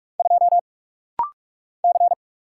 Morse code, heard as a keyed tone, about 44 wpm, 700 Hz. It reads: 2 C
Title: Single Letter-Number Alt - Rapid-Fire 45wpm